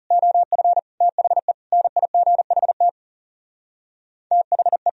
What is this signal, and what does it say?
Morse code, heard as a keyed tone, about 38 wpm, 700 Hz. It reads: OF THE NIGHT THE